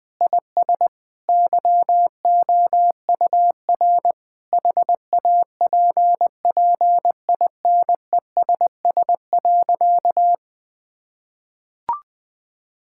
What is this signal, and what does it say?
Morse code, heard as a keyed tone, about 20 wpm, 700 Hz. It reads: IS YOUR HAPPINESS.
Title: Sentences with 6-Word Phrases - Spoken Beginning - 20wpm